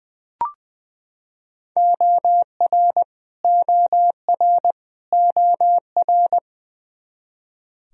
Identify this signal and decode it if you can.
Morse code, heard as a keyed tone, about 20 wpm, 700 Hz. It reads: OR OR OR
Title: Top 100 Words Repeated 20wpm